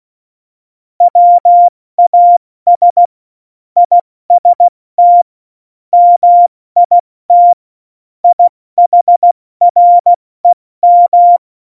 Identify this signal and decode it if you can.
Morse code, heard as a keyed tone, about 16 wpm, spaced as for 14 wpm, 700 Hz. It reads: WAS IST MIT IHREM
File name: Morsekurs / Witze / Witz 07 April 2020, 70 F80